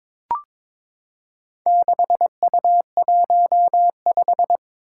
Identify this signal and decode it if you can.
Morse code, heard as a keyed tone, about 22 wpm, 700 Hz. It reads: 6U15
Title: Four Challenging Letter-Number Combinations 22wpm